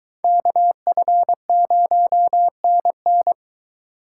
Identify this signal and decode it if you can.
Morse code, heard as a keyed tone, about 23 wpm, 700 Hz. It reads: KF0NN